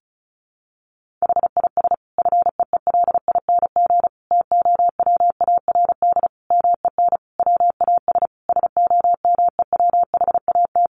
Morse code, heard as a keyed tone, about 34 wpm, 700 Hz. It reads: HIS FEELING TOWARD MEN WAS SOMEWHAT
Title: Sentences from Top 1300 Words 35wpm